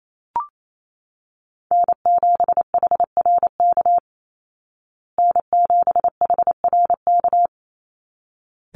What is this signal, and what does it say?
Morse code, heard as a keyed tone, about 28 wpm, 700 Hz. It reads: N7HRK N7HRK
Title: Call Signs Repeated 3x - 28wpm